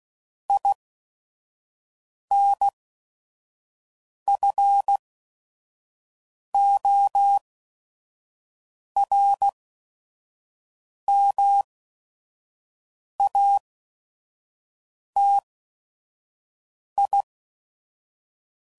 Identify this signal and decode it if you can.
Morse code, heard as a keyed tone, about 16 wpm, spaced as for 5 wpm, 800 Hz. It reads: INFORMATI
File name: CW Code Groups @5WPM / Technical Subjects / SwPwrSup